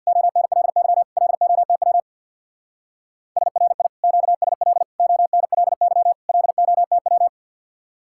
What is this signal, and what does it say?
Morse code, heard as a keyed tone, about 60 wpm, 700 Hz. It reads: ONLY LOTW SRI QSL ONLY LOTW
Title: Speed-Racing Common QSO Phrases - 50wpm